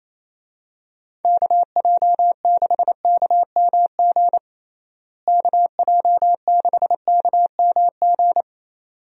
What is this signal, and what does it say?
Morse code, heard as a keyed tone, about 28 wpm, 700 Hz. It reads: KJ6KMG KJ6KMG
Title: USA Call Signs - 6 Characters - Repeated 2x - 28wpm